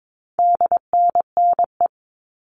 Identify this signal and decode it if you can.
Morse code, heard as a keyed tone, about 22 wpm, 700 Hz. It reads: DNNE
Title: All Cut Numbers - 4-Digit 22wpm